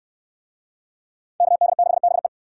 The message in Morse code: KD6BE

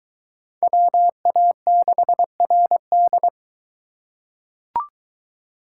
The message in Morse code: WA6RD

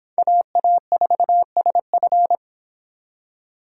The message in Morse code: AA4SF

